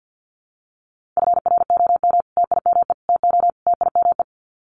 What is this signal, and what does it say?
Morse code, heard as a keyed tone, about 50 wpm, 700 Hz. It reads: FROM TIME TO TIME
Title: Sets of 4 Words 50wpm